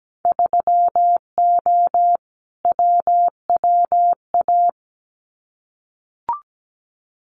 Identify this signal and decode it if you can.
Morse code, heard as a keyed tone, about 17 wpm, 700 Hz. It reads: 3O WWA
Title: Winter Field Day Exchanges Encore 17wpm